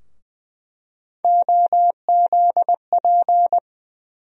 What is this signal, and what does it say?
Morse code, heard as a keyed tone, about 20 wpm, 700 Hz. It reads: OZP
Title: USA Call Sign Postfixes - 3 Letters 20wpm